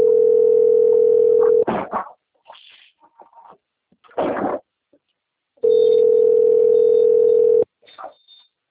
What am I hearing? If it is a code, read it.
Ringback tone (North America), 440 + 480 Hz.
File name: rg-1600-unknown- - - .7452